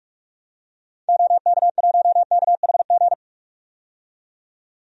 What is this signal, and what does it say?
Morse code, heard as a keyed tone, about 47 wpm, 700 Hz. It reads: OK1KHG
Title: DXCC-100 Call Signs - 6 Characters - 45wpm